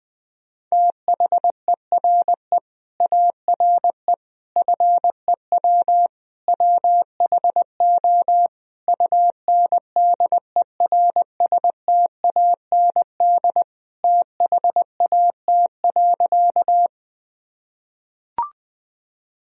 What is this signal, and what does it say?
Morse code, heard as a keyed tone, about 21 wpm, 700 Hz. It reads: THERE ARE FEW WHO UNDERSTAND THAT.
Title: Sentences from Top 1000 Words - Review Part 1 - 20wpm